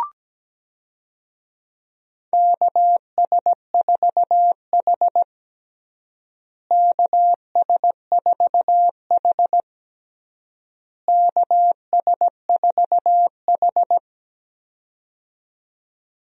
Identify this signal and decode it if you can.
Morse code, heard as a keyed tone, about 17 wpm, 700 Hz. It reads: KS4H KS4H KS4H
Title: USA Call Signs - 4 Characters - Repeated 3x - 17wpm